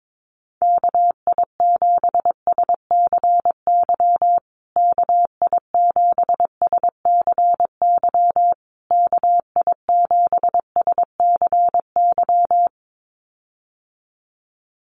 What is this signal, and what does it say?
Morse code, heard as a keyed tone, about 22 wpm, 700 Hz. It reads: KI7SCY KI7SCY KI7SCY
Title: USA Call Signs Repeated 3x 22wpm